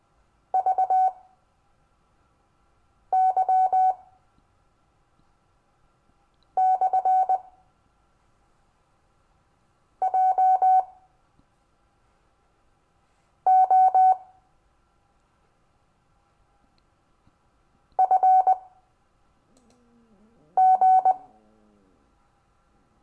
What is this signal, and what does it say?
Morse code, heard as a keyed tone, about 20 wpm, spaced as for 3 wpm, 750 Hz. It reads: VY/JO FG
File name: cw / Lesson15 five forwardslash